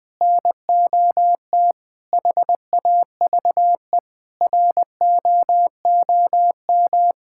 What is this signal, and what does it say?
Morse code, heard as a keyed tone, about 21 wpm, 700 Hz. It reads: NOT HAVE ROOM
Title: Sentences from Top 1000 Words - Review Part 5 - 20wpm